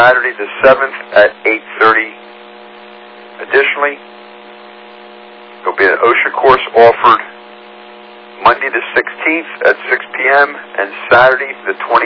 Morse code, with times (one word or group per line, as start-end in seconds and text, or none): none